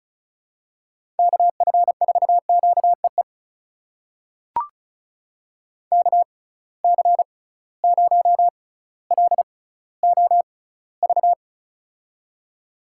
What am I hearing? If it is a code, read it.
Morse code, heard as a keyed tone, about 36 wpm, spaced as for 18 wpm, 700 Hz. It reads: KF4QEE KC0LOV
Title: USA Call Signs - 6 Characters - 6x Character Spacing - 35wpm